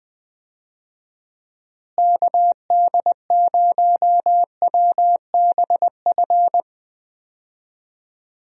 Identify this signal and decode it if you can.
Morse code, heard as a keyed tone, about 21 wpm, 700 Hz. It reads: KD0WBF
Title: USA Call Signs - 6 Characters - Repeated 3x - 20wpm